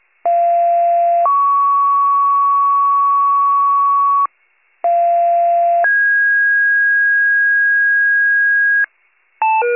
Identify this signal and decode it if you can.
Two-tone page, heard as a one-second tone, then a three-second tone, about 690 then 1120 Hz, about 690 then 1650 Hz, with noise. It is heard from 0.2 to 8.8 s.